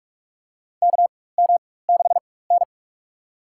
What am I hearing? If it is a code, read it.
Morse code, heard as a keyed tone, about 42 wpm, spaced as for 20 wpm, 700 Hz. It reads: KM6N
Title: USA Call Signs - 4 Characters - 4x Character Spacing - 45wpm